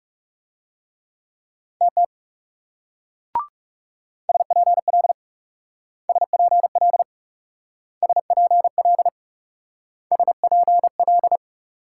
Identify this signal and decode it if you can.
Morse code, heard as a keyed tone, about 41 wpm, 700 Hz. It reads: TT SPL SPL SPL SPL